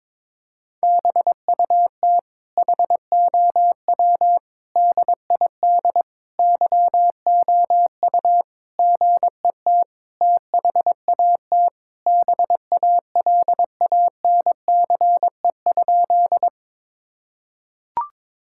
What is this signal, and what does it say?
Morse code, heard as a keyed tone, about 22 wpm, 700 Hz. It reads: BUT HOW DID YOU GET THAT BALANCE?
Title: Sentences from Top 1000 Words - Review Part 6 - 22wpm